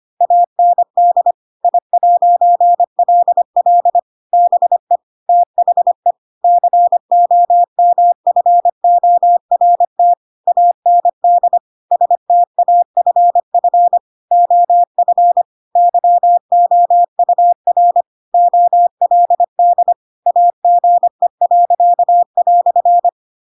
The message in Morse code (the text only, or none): AND I'LL BE THE COMFORT AND STAFF OF YOUR OLD AGE."